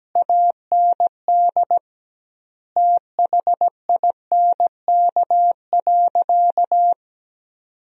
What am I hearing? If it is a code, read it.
Morse code, heard as a keyed tone, about 17 wpm, 700 Hz. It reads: AND THINK.
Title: 2x Word-Spacing - Sentences from Top 500 Words - 17wpm